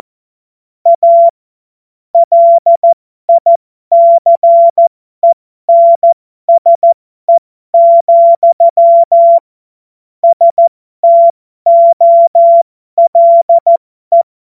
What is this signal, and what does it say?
Morse code, heard as a keyed tone, about 14 wpm, spaced as for 12 wpm, 680 Hz. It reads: A LICENSE, STOLE